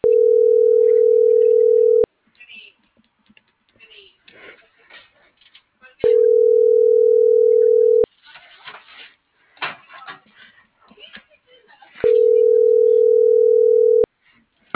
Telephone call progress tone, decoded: ringback tone

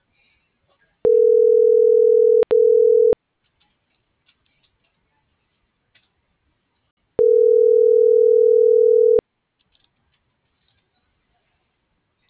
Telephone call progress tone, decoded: ringback tone

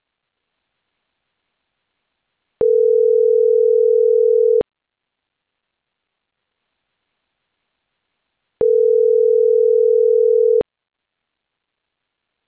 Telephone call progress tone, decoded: ringback tone

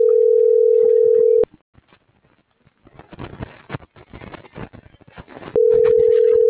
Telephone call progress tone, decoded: ringback tone

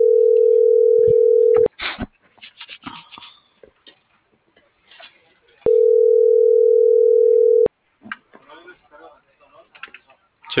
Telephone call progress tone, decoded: ringback tone

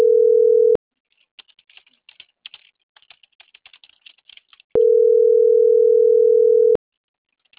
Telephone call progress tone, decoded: ringback tone

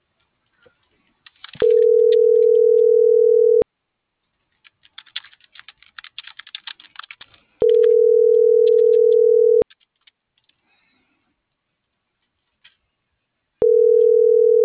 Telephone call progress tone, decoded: ringback tone

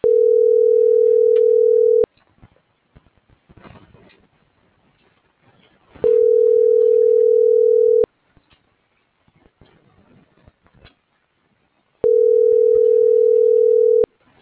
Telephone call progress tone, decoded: ringback tone

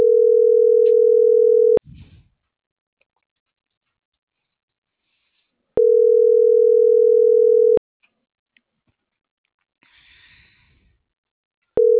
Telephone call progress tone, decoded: ringback tone

